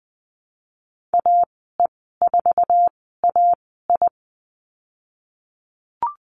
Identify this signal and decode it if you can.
Morse code, heard as a keyed tone, about 20 wpm, spaced as for 14 wpm, 700 Hz. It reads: AE4AI
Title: 055 Lesson 10g - Letter r - Call Signs 20wpm